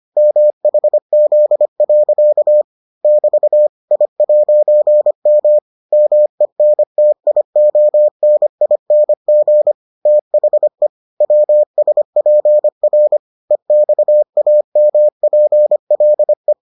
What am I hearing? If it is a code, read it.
Morse code, heard as a keyed tone, about 25 wpm, 600 Hz. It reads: MHZ. <BT> I'M MENTIONING THE WSPR EXAMPLE